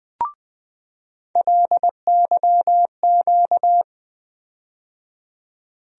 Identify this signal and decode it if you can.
Morse code, heard as a keyed tone, about 20 wpm, 700 Hz. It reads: LYQ
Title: USA Call Sign Postfixes - 3 Letters 20wpm